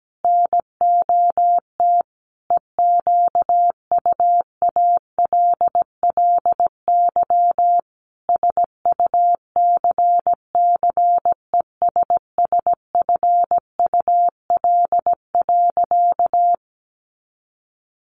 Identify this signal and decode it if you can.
Morse code, heard as a keyed tone, about 17 wpm, 700 Hz. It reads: NOT EQUALLY SUCCESSFUL.
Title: Sentences from Top 1400 Words 17wpm